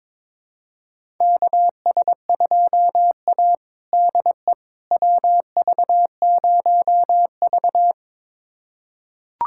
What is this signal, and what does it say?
Morse code, heard as a keyed tone, about 22 wpm, 700 Hz. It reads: KS2A DE WV0V
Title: USA Callsign DE Callsign - 4 Characters - 22wpm